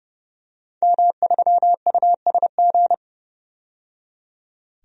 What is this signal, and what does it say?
Morse code, heard as a keyed tone, about 30 wpm, 700 Hz. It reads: M3USG
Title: DXCC-100 Call Signs - 5 Characters - 30wpm